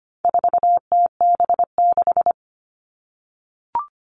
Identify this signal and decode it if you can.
Morse code, heard as a keyed tone, about 25 wpm, 700 Hz. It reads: 4TB6
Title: All Cut Numbers - 4-Digit 25wpm